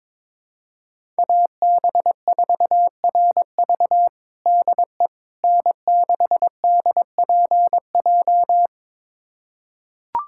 Morse code, heard as a keyed tone, about 22 wpm, 700 Hz. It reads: AB4RV DE N6DPJ
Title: USA Callsign DE Callsign - 5 Characters - 22wpm